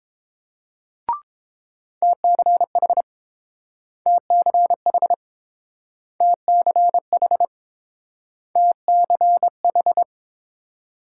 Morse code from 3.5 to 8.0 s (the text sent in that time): TCH TCH